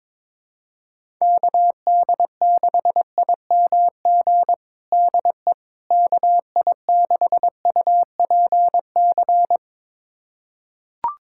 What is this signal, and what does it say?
Morse code, heard as a keyed tone, about 22 wpm, 700 Hz. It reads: KD6IMG DE KI6UPC
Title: USA Callsign DE Callsign 22wpm